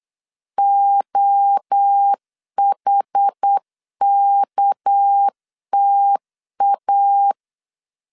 Morse code, heard as a keyed tone, about 8 wpm, 800 Hz. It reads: OHKTA
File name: LEZIONI / 6 PKX0 / 6 PK 8wpm 3003